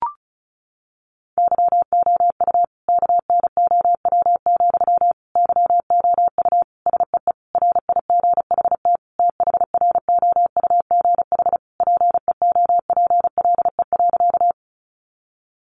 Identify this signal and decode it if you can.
Morse code, heard as a keyed tone, about 34 wpm, 700 Hz. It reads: YOU KNOW, YOU SEE RIGHT THROUGH PEOPLE.
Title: Sentences from Top 200 Words 35wpm